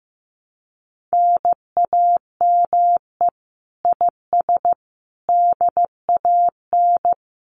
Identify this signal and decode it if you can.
Morse code, heard as a keyed tone, about 16 wpm, 700 Hz. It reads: NAME IS DAN
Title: Common QSO Phrases Encore 15wpm